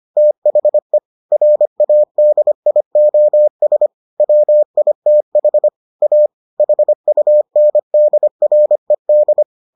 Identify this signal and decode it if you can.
Morse code, heard as a keyed tone, about 25 wpm, 600 Hz. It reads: THE RADIOS WITH A HUNDRED